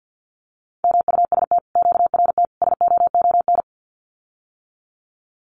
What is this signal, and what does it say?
Morse code, heard as a keyed tone, about 50 wpm, 700 Hz. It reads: MUST QRT SOON
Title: Common QSO Phrases Encore 50wpm